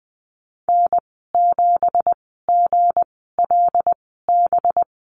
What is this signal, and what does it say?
Morse code, heard as a keyed tone, about 20 wpm, spaced as for 14 wpm, 700 Hz. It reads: N7GLB
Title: 193 Lesson 31g - Letter v - Call Signs 20wpm